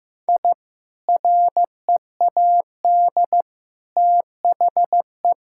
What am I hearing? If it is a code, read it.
Morse code, heard as a keyed tone, about 15 wpm, 700 Hz. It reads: I READ THE
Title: Sentences from Top 1000 Words - Review Part 6 - 15wpm